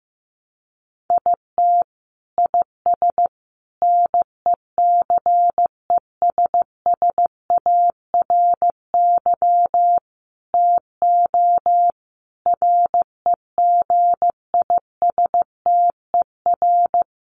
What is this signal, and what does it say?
Morse code, heard as a keyed tone, about 15 wpm, 700 Hz. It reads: IT IS NECESSARY TO REGISTER
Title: Sentences from Top 1200 Words 15wpm